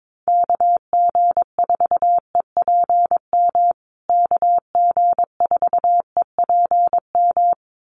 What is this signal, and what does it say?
Morse code, heard as a keyed tone, about 22 wpm, 700 Hz. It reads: KG4EPM KG4EPM
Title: USA Call Signs Repeated 2x 22wpm